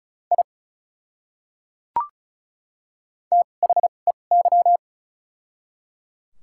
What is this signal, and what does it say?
Morse code, heard as a keyed tone, about 36 wpm, spaced as for 25 wpm, 700 Hz. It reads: I THEY
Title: 2x Character Spacing - Top 100 Words - 35wpm